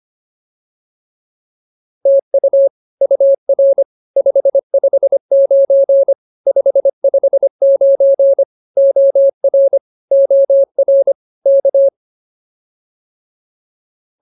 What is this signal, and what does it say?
Morse code, heard as a keyed tone, about 24 wpm, 550 Hz. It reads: TU UR 559 559 OR OR K